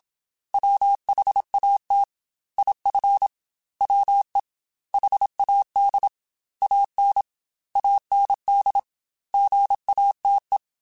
Morse code, heard as a keyed tone, about 27 wpm, 800 Hz. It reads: WHAT IF WE HAD AN AND GATE
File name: Technical Subjects / Technical Subjects-mp3s / AND Gate @25